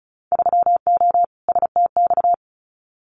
Morse code, heard as a keyed tone, about 34 wpm, 700 Hz. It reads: 3O STX
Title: Winter Field Day Exchanges 35wpm